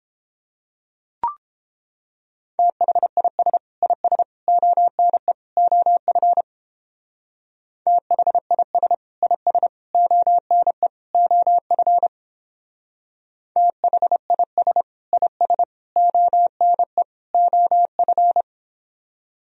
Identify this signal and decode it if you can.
Morse code, heard as a keyed tone, about 29 wpm, 700 Hz. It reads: THIS IS ONE OF THIS IS ONE OF THIS IS ONE OF